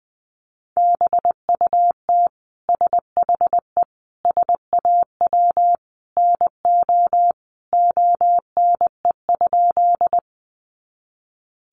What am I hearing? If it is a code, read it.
Morse code, heard as a keyed tone, about 20 wpm, 700 Hz. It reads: BUT SHE SAW NO ONE?
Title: Sentences from Top 300 Words 20wpm